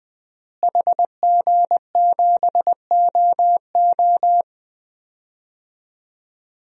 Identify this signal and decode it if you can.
Morse code, heard as a keyed tone, about 20 wpm, 700 Hz. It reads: HG7OO
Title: DXCC-100 Call Signs - 5 Characters - Repeated 3x - 20wpm